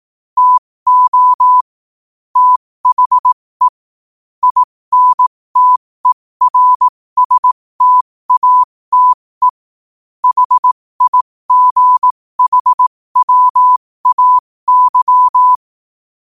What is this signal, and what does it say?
Morse code, heard as a keyed tone, about 18 wpm, spaced as for 15 wpm, 1000 Hz. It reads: TO THE INTERSTATE HIGHWAY
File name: qotdfiles / 15wpm